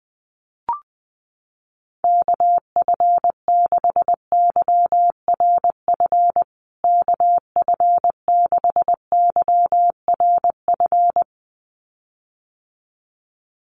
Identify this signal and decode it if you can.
Morse code, heard as a keyed tone, about 20 wpm, 700 Hz. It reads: KF6YRF KF6YRF
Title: USA Call Signs Repeated 2x 20wpm